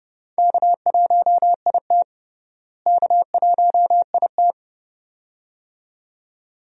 Morse code, heard as a keyed tone, about 30 wpm, 700 Hz. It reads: K1IT K1IT
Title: USA Call Signs - 4 Characters - Repeated 2x - 30wpm